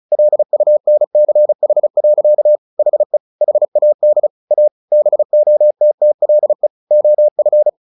Morse code, heard as a keyed tone, about 35 wpm, 600 Hz. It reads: LUNCH. HE HAD A BOTTLE OF
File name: the old man and the sea / the old man and the sea 1 35wpm